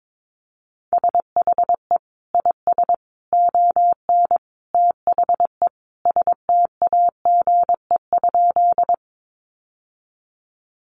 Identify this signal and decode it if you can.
Morse code, heard as a keyed tone, about 22 wpm, 700 Hz. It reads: SHE IS ON THE STAGE?
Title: Sentences from Top 1200 Words 22wpm